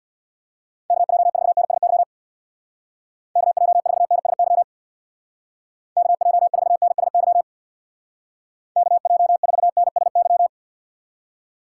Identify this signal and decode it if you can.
Morse code, heard as a keyed tone, about 59 wpm, 700 Hz. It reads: KJ4NSY KJ4NSY KJ4NSY KJ4NSY
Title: Speed Racing - USA Call Signs 50wpm